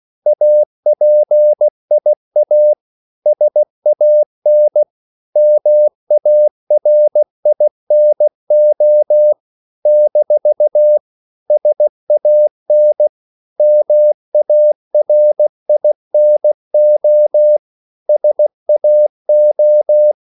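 Morse code, heard as a keyed tone, about 16 wpm, 600 Hz. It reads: APIA SAN MARINO - SAN MARINO SAO
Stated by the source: CW practice of national capitals at 16 words per minute, from ON6ZQ.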